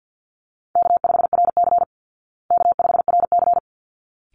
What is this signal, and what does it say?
Morse code, heard as a keyed tone, about 50 wpm, 700 Hz. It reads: K5RC K5RC
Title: Easy Call Signs Repeated 2x - 50wpm